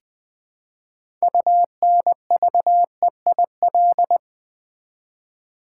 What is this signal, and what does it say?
Morse code, heard as a keyed tone, about 20 wpm, 700 Hz. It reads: UNVEIL
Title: Build A Word - 3 Letters At A Time - 6-Letter Words - 20wpm